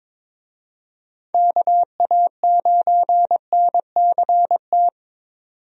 Morse code, heard as a keyed tone, about 22 wpm, 700 Hz. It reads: KA9NCT
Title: USA Call Signs - 6 Characters - 22wpm